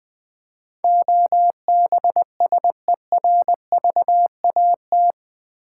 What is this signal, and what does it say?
Morse code, heard as a keyed tone, about 20 wpm, 700 Hz. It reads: OBSERVAT
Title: Build A Word - 4 Letters At A Time - 11-Letter Words - 20wpm